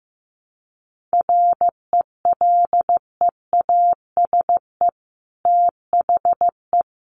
Morse code, heard as a keyed tone, about 15 wpm, 700 Hz. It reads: RELEASE THE
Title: Sentences from Top 1100 Words 15wpm